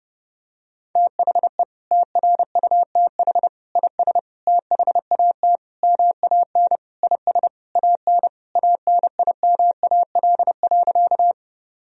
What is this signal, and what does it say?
Morse code, heard as a keyed tone, about 32 wpm, 700 Hz. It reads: THE TRUTH IS THAT MAN IS AN ANIMAL.
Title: Sentences from Top 1000 Words - Review Part 1 - 30wpm